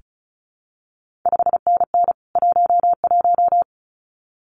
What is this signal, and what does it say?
Morse code, heard as a keyed tone, about 36 wpm, 700 Hz. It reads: 5NN 11
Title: CQ WW DX Contest Exchange Encore - 35wpm